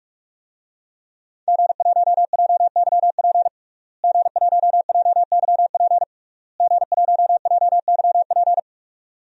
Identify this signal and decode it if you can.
Morse code, heard as a keyed tone, about 44 wpm, 700 Hz. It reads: G1JYP G1JYP G1JYP
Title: DXCC-100 Call Signs - 5 Characters - Repeated 3x - 45wpm